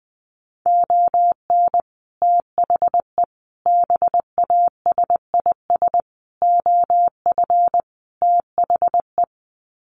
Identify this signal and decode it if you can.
Morse code, heard as a keyed tone, about 20 wpm, 700 Hz. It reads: ON THE BASIS OF THE
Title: Sets of 5 Words 20wpm